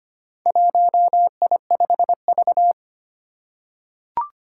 Morse code, heard as a keyed tone, about 25 wpm, 700 Hz. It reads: 1I5V
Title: Four Challenging Letter-Number Combinations 25wpm